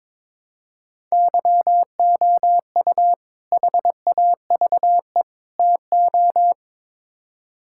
Morse code, heard as a keyed tone, about 22 wpm, 700 Hz. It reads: YOU HAVE TO
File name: practice / 22wpm / Build A Phrase - 5-Word Phrases - 22wpm